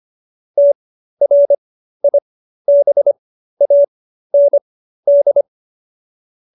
Morse code, heard as a keyed tone, about 25 wpm, spaced as for 13 wpm, 570 Hz. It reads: TRIBAND